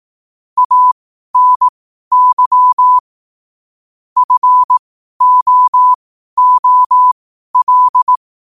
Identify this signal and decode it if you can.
Morse code, heard as a keyed tone, about 18 wpm, spaced as for 12 wpm, 1000 Hz. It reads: ANY FOOL
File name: qotdfiles / 12wpm